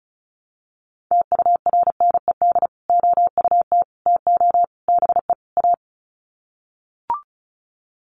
Morse code, heard as a keyed tone, about 34 wpm, 700 Hz. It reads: TURNED OUT TO BE A
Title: Sets of 5 Words 35wpm